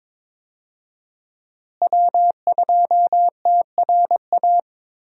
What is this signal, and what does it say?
Morse code, heard as a keyed tone, about 22 wpm, 700 Hz. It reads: W2TRA
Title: USA Call Signs - 5 Characters - Repeated 3x - 22wpm